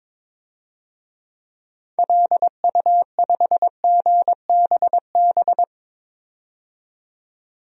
Morse code, heard as a keyed tone, about 22 wpm, 700 Hz. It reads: LU5GBB
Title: DXCC-100 Call Signs - 6 Characters - 22wpm